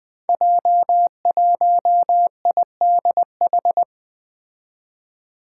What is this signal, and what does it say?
Morse code, heard as a keyed tone, about 20 wpm, 700 Hz. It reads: J1IDH
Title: Five Challenging Letter-Number Combinations 20wpm